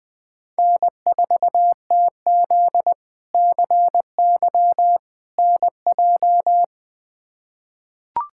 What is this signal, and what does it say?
Morse code, heard as a keyed tone, about 20 wpm, 700 Hz. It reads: N4TZ CY NJ
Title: K1USN SST QSO Practice - 20wpm